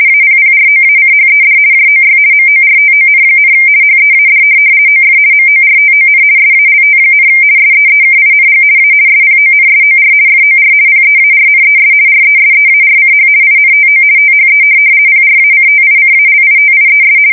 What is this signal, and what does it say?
RTTY, about 50 baud. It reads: APPROACHED JOY MORTON, HEAD OF MORTON SALT, SEEKING A SPONSOR FOR PEARNES RESEARCH INTO THE PRACTICALITIES
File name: fsk170-50B/teletype-corp